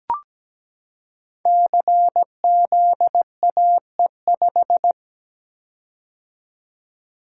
Morse code, heard as a keyed tone, about 17 wpm, 700 Hz. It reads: CZAE5
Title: Code Groups - 5 Letter:Numbers - 17wpm